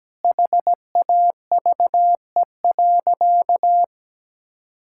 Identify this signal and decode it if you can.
Morse code, heard as a keyed tone, about 17 wpm, 700 Hz. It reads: HAVE.
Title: Sentences from Top 1000 Words - Review Part 5 - 17wpm